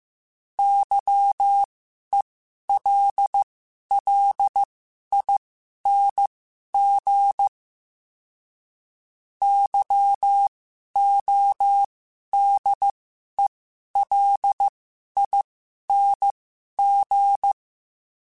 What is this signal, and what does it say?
Morse code, heard as a keyed tone, about 15 wpm, spaced as for 11 wpm, 800 Hz. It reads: YELLING YODELING
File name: Special INGs-mp3s / Special INGs-mp3s / Sngl Ing @10